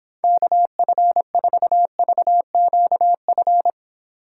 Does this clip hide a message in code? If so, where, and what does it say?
Yes: Morse code from 0.2 to 3.7 s, heard as a keyed tone, about 31 wpm, 700 Hz. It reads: KF4VQF